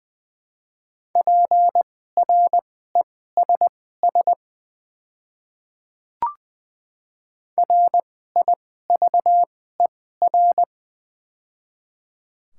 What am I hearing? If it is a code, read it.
Morse code, heard as a keyed tone, about 20 wpm, spaced as for 15 wpm, 700 Hz. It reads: PRESS RIVER
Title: 2x Character Spacing - Top 300 Words - 20wpm